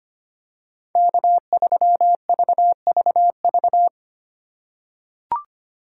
Morse code, heard as a keyed tone, about 25 wpm, 700 Hz. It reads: K3VVV
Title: USA Call Signs - 5 Characters - 2x Character Spacing - 25wpm